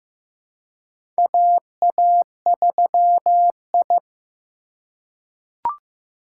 Morse code, heard as a keyed tone, about 15 wpm, 700 Hz. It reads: AA3I
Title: USA Call Signs - 4 Characters - 3x Character Spacing - 15wpm